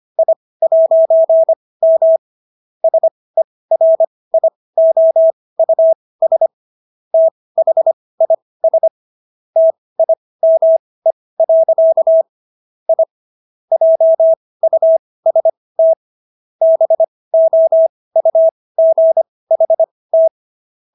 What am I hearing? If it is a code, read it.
Morse code, heard as a keyed tone, about 25 wpm, spaced as for 18 wpm, 650 Hz. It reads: I'M SERIOUS THIS TIME. I JUST BOUGHT